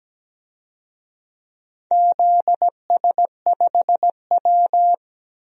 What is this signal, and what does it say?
Morse code, heard as a keyed tone, about 17 wpm, 700 Hz. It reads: ZS5W
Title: DXCC-100 Call Signs - 4 Characters - Repeated 2x - 17wpm